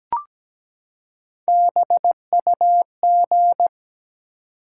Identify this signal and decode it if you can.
Morse code, heard as a keyed tone, about 17 wpm, 700 Hz. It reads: BUG
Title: Three Letter Words - 17wpm